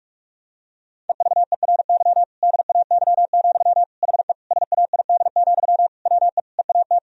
Morse code, heard as a keyed tone, about 46 wpm, 700 Hz. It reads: EVERY DAY, HE SAID, WE EAT
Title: Sentences from Top 1000 Words - Review Part 1 - 45wpm